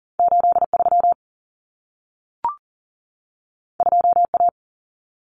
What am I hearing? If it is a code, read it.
Morse code, heard as a keyed tone, about 40 wpm, 700 Hz. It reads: 83 2A
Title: Common Cut Numbers - 2-Digit 40wpm